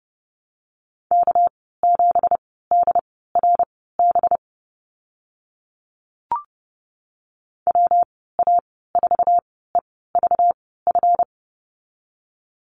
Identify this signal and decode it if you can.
Morse code, heard as a keyed tone, about 30 wpm, spaced as for 17 wpm, 700 Hz. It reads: K7DRB WA4EVF
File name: Lesson 37g - Letter x - Call Signs 30wpm